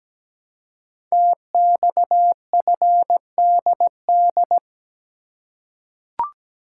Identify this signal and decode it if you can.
Morse code, heard as a keyed tone, about 17 wpm, 700 Hz. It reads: TXFDD
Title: Code Groups - 5 Letters - 17wpm